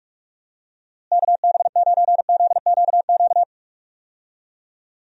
Morse code, heard as a keyed tone, about 45 wpm, 700 Hz. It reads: KB9ZQQ
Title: USA Call Signs - 6 Characters - Repeated 2x - 45wpm